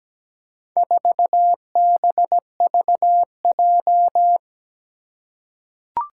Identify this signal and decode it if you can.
Morse code, heard as a keyed tone, about 17 wpm, 700 Hz. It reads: 4BVJ
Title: Four Challenging Letter-Number Combinations 17wpm